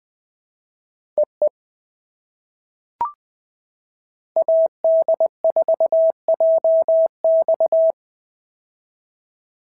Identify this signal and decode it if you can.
Morse code, heard as a keyed tone, about 19 wpm, 650 Hz. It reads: EE AD4JX
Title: POTA QSO Practice - 20wpm